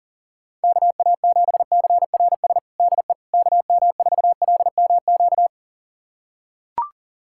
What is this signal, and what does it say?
Morse code, heard as a keyed tone, about 40 wpm, 700 Hz. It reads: KA7CRS DE KM4LMQ
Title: USA Callsign DE Callsign 40wpm